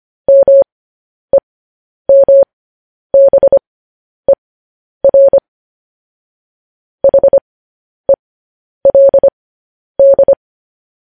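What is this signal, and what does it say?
Morse code, heard as a keyed tone, about 25 wpm, spaced as for 10 wpm, 550 Hz. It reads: MEMBER HELD
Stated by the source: Morse code transmission of news headlines.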